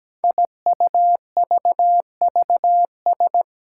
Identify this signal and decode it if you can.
Morse code, heard as a keyed tone, about 17 wpm, 700 Hz. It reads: IUVVS
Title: Five Challenging Letter-Number Combinations 17wpm